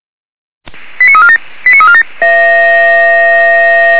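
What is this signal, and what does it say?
Five-tone selective call (ZVEI), heard as quick runs of five tones, tones of about 70 ms. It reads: 89347 89347
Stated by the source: ZVEI five-tone selective call received with an SDR.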